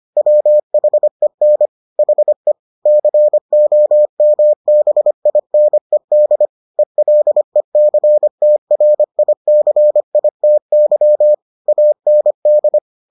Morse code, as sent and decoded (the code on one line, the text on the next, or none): .-- .... -. / .... / -.-. --- -- -... .. -. -.. / . .-.. -.-. - .-. .. -.-. .. - -.-- / .- -. -..
WHEN HE COMBINED ELECTRICITY AND